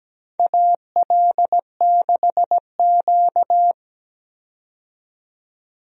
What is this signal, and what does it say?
Morse code, heard as a keyed tone, about 17 wpm, 700 Hz. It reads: AL6Q
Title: USA Call Signs - 4 Characters - 17wpm